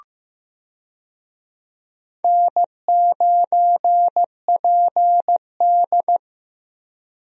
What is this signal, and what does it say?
Morse code, heard as a keyed tone, about 15 wpm, 700 Hz. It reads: N9PD
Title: USA Call Signs - 4 Characters - Repeated 3x - 15wpm